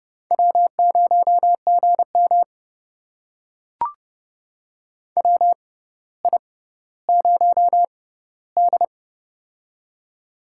Morse code, heard as a keyed tone, about 29 wpm, spaced as for 15 wpm, 700 Hz. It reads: W0GM WI0D
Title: USA Call Signs - 4 Characters - 6x Character Spacing - 30wpm